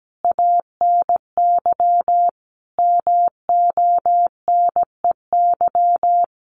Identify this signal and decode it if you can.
Morse code, heard as a keyed tone, about 17 wpm, 700 Hz. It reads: ANY MONEY
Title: Sentences from Top 1400 Words 17wpm